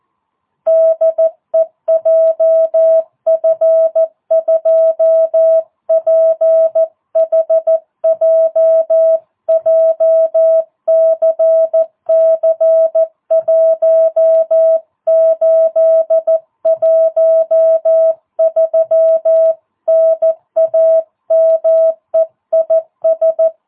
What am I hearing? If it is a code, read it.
Morse code, heard as a keyed tone, about 14 wpm, 650 Hz. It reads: DE JF2PHJ JCC1813 NAME IS
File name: jh8cbh / 2022cw